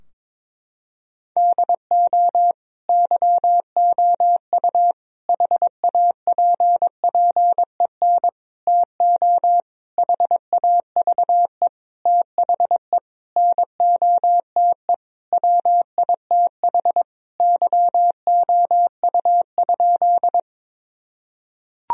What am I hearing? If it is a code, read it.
Morse code, heard as a keyed tone, about 21 wpm, 700 Hz. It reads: DO YOU HAPPEN TO HAVE THE NOTE WITH YOU?
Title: Sentences from Top 1000 Words - Review Part 2 - 22wpm